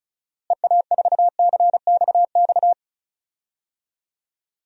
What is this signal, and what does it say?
Morse code, heard as a keyed tone, about 34 wpm, 700 Hz. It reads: EA4CXX
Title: DXCC-100 Call Signs - 6 Characters - Repeated 3x - 35wpm